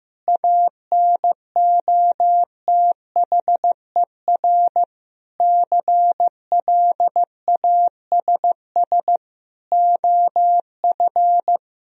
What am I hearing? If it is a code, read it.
Morse code, heard as a keyed tone, about 15 wpm, 700 Hz. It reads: ANOTHER CLASS OF
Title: Sentences from Top 1000 Words - Review Part 2 - 15wpm